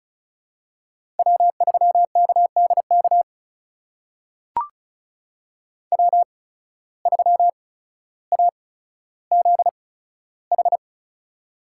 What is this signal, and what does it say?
Morse code, heard as a keyed tone, about 36 wpm, spaced as for 15 wpm, 700 Hz. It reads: W3KDK W3AZH
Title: USA Call Signs - 5 Characters - 8x Character Spacing - 35wpm